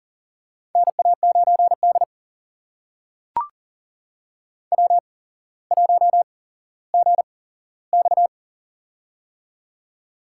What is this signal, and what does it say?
Morse code, heard as a keyed tone, about 40 wpm, spaced as for 17 wpm, 700 Hz. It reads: NA9D W1GX
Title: USA Call Signs - 4 Characters - 8x Character Spacing - 40wpm